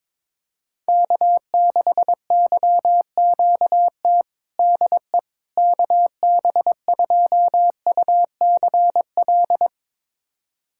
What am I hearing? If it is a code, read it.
Morse code, heard as a keyed tone, about 22 wpm, 700 Hz. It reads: K6YQT DE KB2UCL
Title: USA Callsign DE Callsign 22wpm